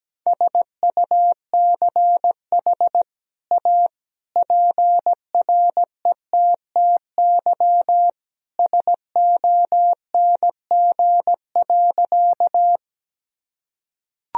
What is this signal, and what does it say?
Morse code, heard as a keyed tone, about 17 wpm, 700 Hz. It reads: SUCH A PRETTY SONG.
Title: Sentences from Top 1000 Words - Review Part 2 - 17wpm